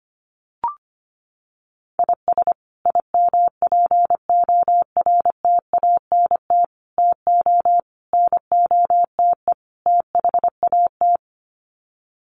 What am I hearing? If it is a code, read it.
Morse code, heard as a keyed tone, about 25 wpm, 700 Hz. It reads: IS IMPORTANT TO NOTE THAT
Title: Sets of 5 Words 25wpm